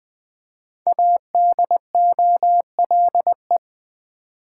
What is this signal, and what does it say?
Morse code, heard as a keyed tone, about 20 wpm, 700 Hz. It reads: ADOLE
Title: Build A Word - 5 Letters At A Time - 10-Letter Words - 20wpm